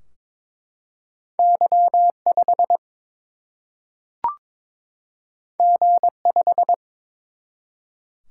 Morse code, heard as a keyed tone, about 22 wpm, 700 Hz. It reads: Y 5 G 5
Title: Single-Letter Single-Number 22wpm